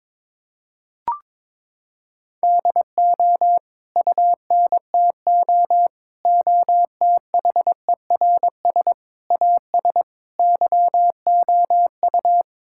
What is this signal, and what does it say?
Morse code, heard as a keyed tone, about 22 wpm, 700 Hz. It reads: DO UNTO OTHERS AS YOU
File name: Sets of 9 Words - 22wpm